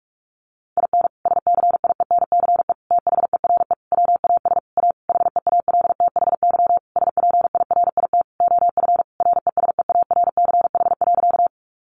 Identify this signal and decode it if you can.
Morse code, heard as a keyed tone, about 44 wpm, 700 Hz. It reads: IN SCIENCE THERE WAS A HEALTHY SPIRIT OF RESEARCH.
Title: Sentences from Top 1400 Words 45wpm